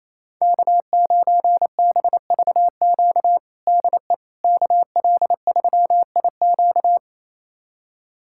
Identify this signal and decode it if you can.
Morse code, heard as a keyed tone, about 28 wpm, 700 Hz. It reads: K9BVQ DE KL3IQ
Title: USA Callsign DE Callsign - 5 Characters - 28wpm